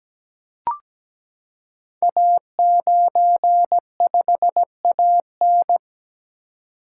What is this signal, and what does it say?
Morse code, heard as a keyed tone, about 17 wpm, 700 Hz. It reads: A95AN
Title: Code Groups - 5 Letter:Numbers - 17wpm